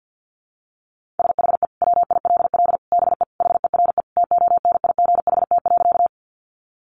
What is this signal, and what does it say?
Morse code, heard as a keyed tone, about 50 wpm, 700 Hz. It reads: SHE WILL BE HERE TONIGHT.
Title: Sentences from Top 600 Words 50wpm